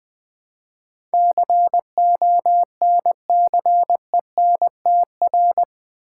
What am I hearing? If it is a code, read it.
Morse code, heard as a keyed tone, about 20 wpm, 700 Hz. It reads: CONCENTR
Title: Build A Word - 4 Letters At A Time - 11-Letter Words - 20wpm